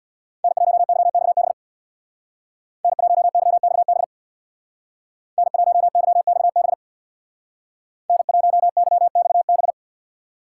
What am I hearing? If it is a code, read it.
Morse code, heard as a keyed tone, about 64 wpm, 700 Hz. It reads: N1YXB N1YXB N1YXB N1YXB